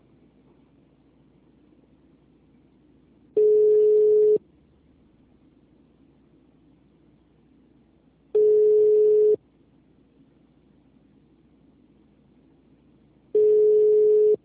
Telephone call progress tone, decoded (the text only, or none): ringback tone